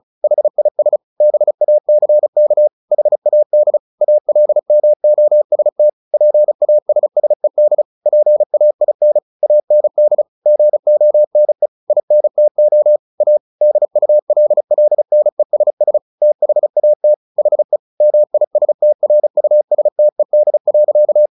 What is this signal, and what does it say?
Morse code, heard as a keyed tone, about 35 wpm, 600 Hz. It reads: HIS BACK HAD ALMOST PASSED PAIN AND GONE INTO A DULLNESS THAT HE MISTRUSTED.
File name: the old man and the sea 2 35wpm